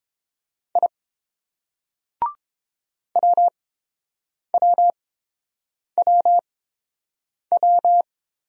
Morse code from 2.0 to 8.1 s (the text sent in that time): WWWW